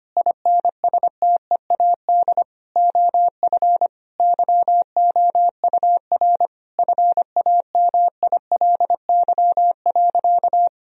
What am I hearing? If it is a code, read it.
Morse code, heard as a keyed tone, about 25 wpm, 700 Hz. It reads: INSTEAD OF YOUR FAMILY.
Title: Sentences from Top 1000 Words - Review Part 4 - 25wpm